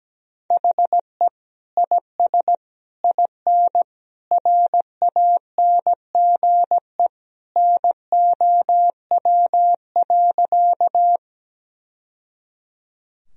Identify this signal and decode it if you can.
Morse code, heard as a keyed tone, about 17 wpm, 700 Hz. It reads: HE IS IN RANGE NOW.
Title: Sentences from Top 1000 Words - Review Part 4 - 17wpm